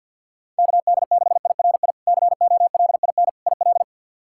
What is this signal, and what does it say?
Morse code, heard as a keyed tone, about 52 wpm, 700 Hz. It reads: KD6IRI COLIN IL
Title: K1USN SST QSO Practice - 50wpm